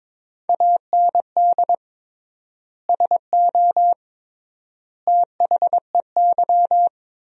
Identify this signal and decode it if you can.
Morse code, heard as a keyed tone, about 22 wpm, 700 Hz. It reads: AND SO THEY
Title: 3x Word-Spacing - Sentences from Top 400 Words - 22wpm